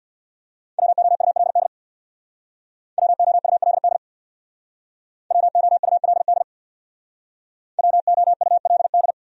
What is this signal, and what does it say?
Morse code, heard as a keyed tone, about 62 wpm, 700 Hz. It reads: WOULD WOULD WOULD WOULD
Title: Speed-Racing Top 300 Words 50wpm